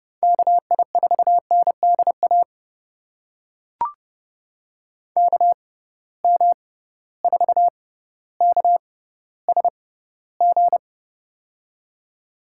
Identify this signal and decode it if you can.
Morse code, heard as a keyed tone, about 30 wpm, spaced as for 15 wpm, 700 Hz. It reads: KI4NDA KM4KSG
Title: USA Call Signs - 6 Characters - 6x Character Spacing - 30wpm